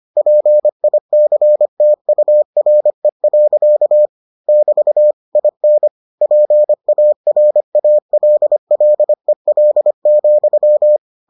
Morse code, heard as a keyed tone, about 25 wpm, 600 Hz. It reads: PICTURE. <BT> IN PARALLEL,